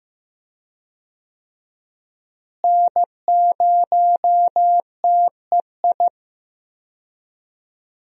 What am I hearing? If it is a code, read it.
Morse code, heard as a keyed tone, about 15 wpm, 700 Hz. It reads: N0TEI
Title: USA Call Signs - 5 Characters - Repeated 3x - 15wpm